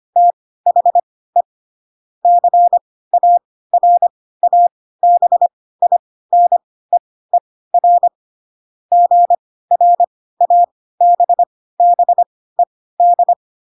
Morse code, heard as a keyed tone, about 25 wpm, spaced as for 16 wpm, 700 Hz. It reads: THE CARABINEER GRABBED
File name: Pinocchio / Pinocchio ch3 10 25 ik1hge